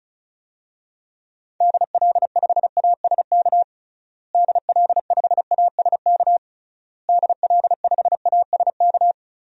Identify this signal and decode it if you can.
Morse code, heard as a keyed tone, about 36 wpm, 700 Hz. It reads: DL5ASK DL5ASK DL5ASK
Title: DXCC-100 Call Signs - 6 Characters - Repeated 3x - 35wpm